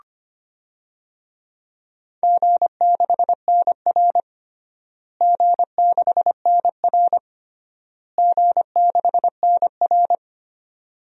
Morse code, heard as a keyed tone, about 25 wpm, 700 Hz. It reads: G6NR G6NR G6NR
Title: DXCC-100 Call Signs - 4 Characters - Repeated 3x - 25wpm